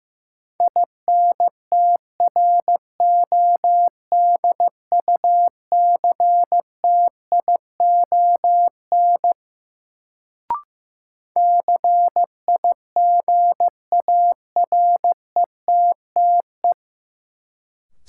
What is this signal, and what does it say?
Morse code, heard as a keyed tone, about 15 wpm, 700 Hz. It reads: INTRODUCTION CIGARETTE
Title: Sentences from Top 1000 Words - New Word Drill - 15wpm